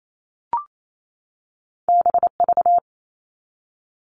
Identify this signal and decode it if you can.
Morse code, heard as a keyed tone, about 28 wpm, 700 Hz. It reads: BV
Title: All Cut Numbers - 2-Digit 28wpm